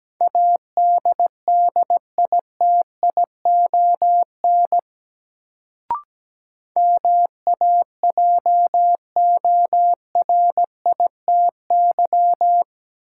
Morse code, heard as a keyed tone, about 17 wpm, 700 Hz. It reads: ADDITION MAJORITY
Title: Sets of 3 Words - New Word Drill - 17wpm